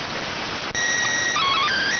SSTV (slow-scan television), PD 290.